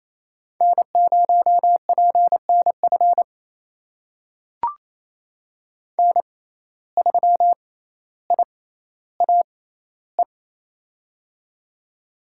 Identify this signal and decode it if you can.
Morse code, heard as a keyed tone, about 28 wpm, spaced as for 14 wpm, 700 Hz. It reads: N0PNF N3IAE
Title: USA Call Signs - 5 Characters - 6x Character Spacing - 28wpm